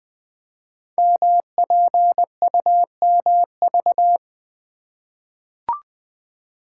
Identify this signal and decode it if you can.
Morse code, heard as a keyed tone, about 20 wpm, 700 Hz. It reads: MPUMV
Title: Code Groups - 5 Letters - 20wpm